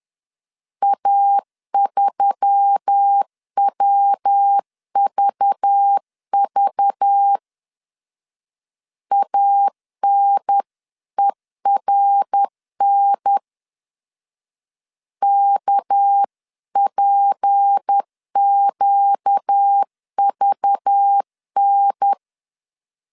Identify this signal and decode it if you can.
Morse code, heard as a keyed tone, about 10 wpm, 800 Hz. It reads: A3WVV ANERN KPQVN